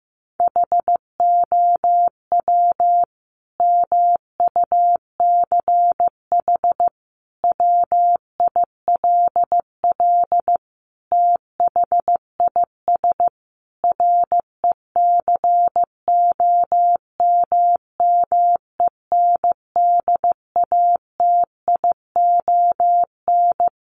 Morse code, heard as a keyed tone, about 15 wpm, 700 Hz. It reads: HOW MUCH WILL THIS RECOMMENDATION
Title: Sentences from Top 1400 Words 15wpm